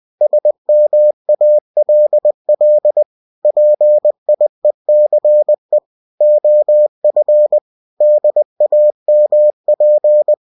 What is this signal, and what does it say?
Morse code, heard as a keyed tone, about 20 wpm, 600 Hz. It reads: SMALL PIECE OF DAMP
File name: cw-books / winnie-the-pooh / winnie-20wpm-page-0053